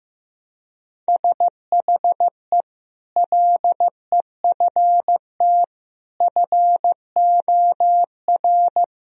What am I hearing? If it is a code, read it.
Morse code, heard as a keyed tone, about 15 wpm, 700 Hz. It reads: SHE LEFT FOR
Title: Sentences from Top 1000 Words - Review Part 3 - 15wpm